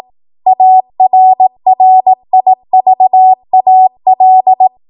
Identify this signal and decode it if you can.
Morse code, heard as a keyed tone, about 18 wpm, 750 Hz. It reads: ARRIVAL
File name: Morse / 18WPM